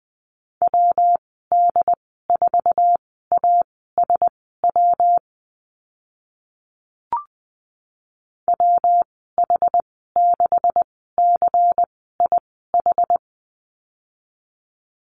Morse code, heard as a keyed tone, about 20 wpm, spaced as for 14 wpm, 700 Hz. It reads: WD4ASW WH6CIH
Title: 140 Lesson 23g - Letter f - Call Signs 20wpm